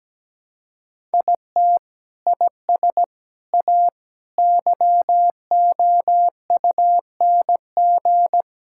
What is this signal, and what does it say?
Morse code, heard as a keyed tone, about 17 wpm, 700 Hz. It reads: IT IS A YOUNG
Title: Sentences from Top 1000 Words - Review Part 6 - 17wpm